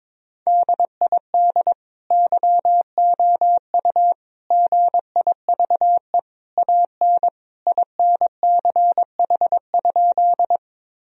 Morse code, heard as a keyed tone, about 22 wpm, 700 Hz. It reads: DID YOU GIVE AN INCH?
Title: Sentences from Top 1000 Words - Review Part 3 - 22wpm